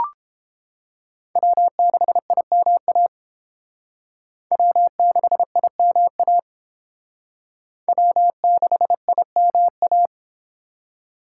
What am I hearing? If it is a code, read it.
Morse code, heard as a keyed tone, about 29 wpm, 700 Hz. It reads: W6IMA W6IMA W6IMA